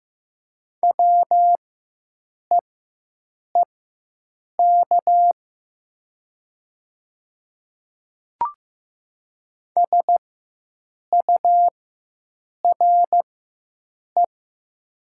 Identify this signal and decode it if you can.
Morse code, heard as a keyed tone, about 15 wpm, spaced as for 7 wpm, 700 Hz. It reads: WEEK SURE
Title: 4x Character Spacing - Top 500 Words - 15wpm